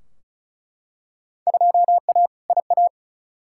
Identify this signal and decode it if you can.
Morse code, heard as a keyed tone, about 36 wpm, 700 Hz. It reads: 2A IA
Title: Field Day Exchanges - 35wpm